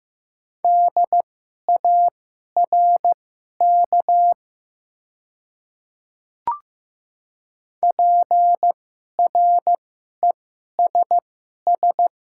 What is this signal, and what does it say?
Morse code, heard as a keyed tone, about 15 wpm, spaced as for 11 wpm, 700 Hz. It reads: DARK PRESS
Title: 2x Character Spacing - Top 500 Words - 15wpm